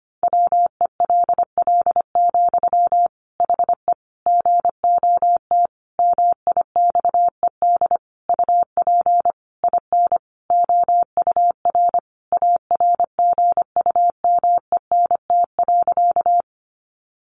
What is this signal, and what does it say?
Morse code, heard as a keyed tone, about 25 wpm, 700 Hz. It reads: WELL, HE GOT MIXED UP IN OUR ARGUMENT.
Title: Sentences from Top 1500 Words 25wpm